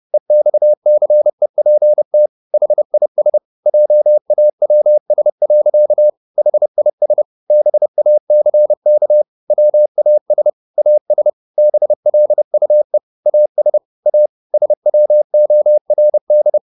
Morse code, heard as a keyed tone, about 30 wpm, 600 Hz. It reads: EXCEPT HIS JAWS. HIS BACK WAS AS BLUE AS A SWORD